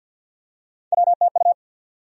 Morse code, heard as a keyed tone, about 53 wpm, 700 Hz. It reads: WTU
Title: USA Call Sign Postfixes - 3 Letters 50wpm